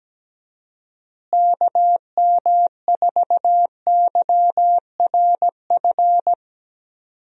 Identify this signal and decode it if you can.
Morse code, heard as a keyed tone, about 17 wpm, 700 Hz. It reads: KM4YRF
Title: USA Call Signs - 6 Characters - Repeated 3x - 17wpm